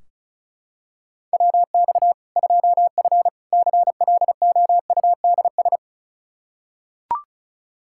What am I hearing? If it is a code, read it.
Morse code, heard as a keyed tone, about 36 wpm, 700 Hz. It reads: WX 2F CLOUDS
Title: Practice QSO - Exchange 2 - Info - 35wpm